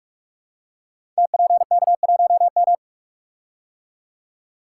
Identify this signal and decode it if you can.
Morse code, heard as a keyed tone, about 44 wpm, 700 Hz. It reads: TPK1M
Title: Code Groups - 5 Letter:Numbers - 45wpm